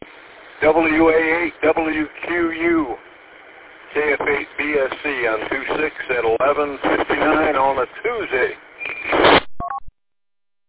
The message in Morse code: none

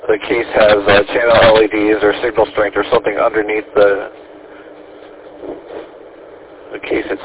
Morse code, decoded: none

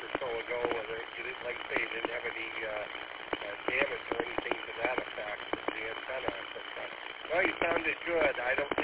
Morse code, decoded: none

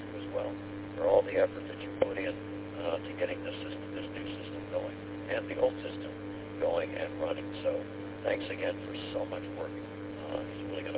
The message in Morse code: none

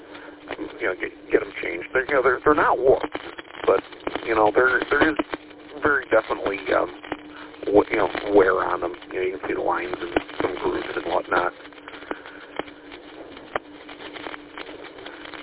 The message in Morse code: none

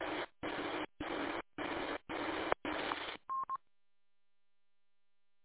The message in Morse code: none

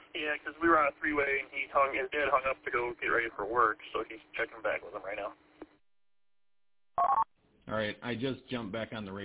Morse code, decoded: none

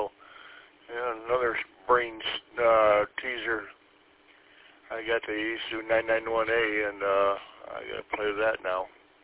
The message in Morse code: none

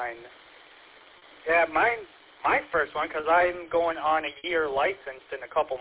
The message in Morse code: none

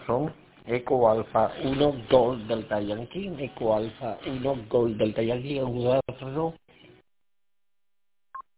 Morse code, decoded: none

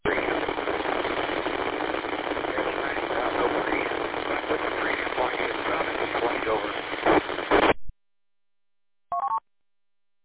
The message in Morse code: none

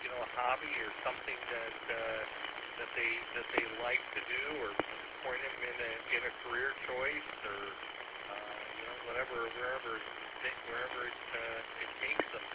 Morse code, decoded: none

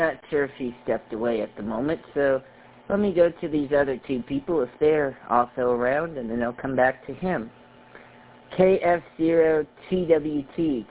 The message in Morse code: none

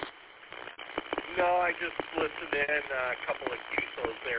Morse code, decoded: none

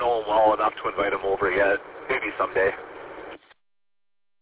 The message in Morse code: none

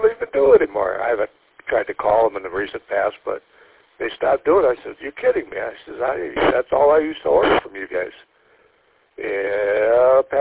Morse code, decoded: none

